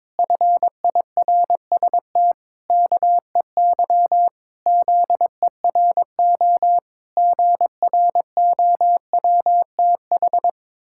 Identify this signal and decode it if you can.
Morse code, heard as a keyed tone, about 22 wpm, 700 Hz. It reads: FIRST KEY ZERO GROWTH
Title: Pangram Challenge v2 - 22wpm